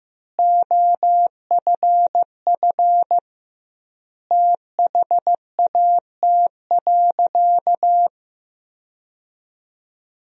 Morse code, heard as a keyed tone, about 15 wpm, 700 Hz. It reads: OFF THAT.
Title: 2x Word-Spacing - Sentences from Top 400 Words - 15wpm